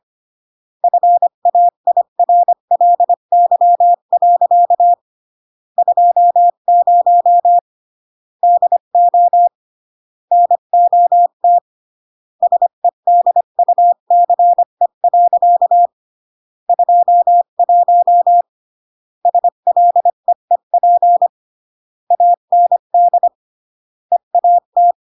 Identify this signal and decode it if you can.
Morse code, heard as a keyed tone, about 25 wpm, 700 Hz. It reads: FAIRLY. 20 DO NOT SEDUCE. 21 SLEEP AND EAT
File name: CW / jodorowsky / 82 commandments 25-20